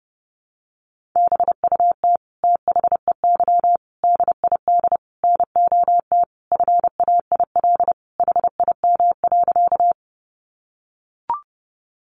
Morse code, heard as a keyed tone, about 30 wpm, 700 Hz. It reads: BUT THEY DID NOT FAIL HIM.
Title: Sentences from Top 1100 Words 30wpm